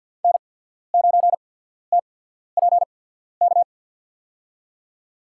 Morse code, heard as a keyed tone, about 53 wpm, spaced as for 14 wpm, 700 Hz. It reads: N9TPK
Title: USA Call Signs - 5 Characters - 8x Character Spacing - 50wpm